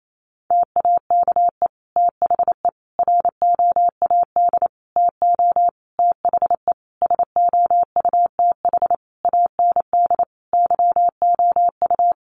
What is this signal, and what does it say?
Morse code, heard as a keyed tone, about 28 wpm, 700 Hz. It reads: TAKE THE ROAD TO THE SOUTH AND YOU
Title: Sentences from Top 400 Words 28wpm